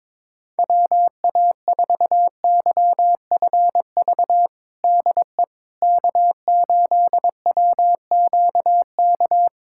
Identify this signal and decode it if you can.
Morse code, heard as a keyed tone, about 22 wpm, 700 Hz. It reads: WA4YFV DE K8WQK
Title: USA Callsign DE Callsign 22wpm